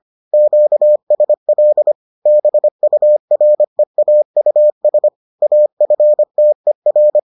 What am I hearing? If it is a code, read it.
Morse code, heard as a keyed tone, about 25 wpm, 600 Hz. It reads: QSL BUREAUS AFTER